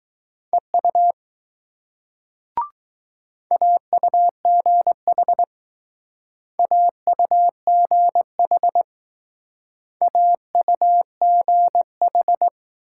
Morse code, heard as a keyed tone, about 21 wpm, 700 Hz. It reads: EU AUGH AUGH AUGH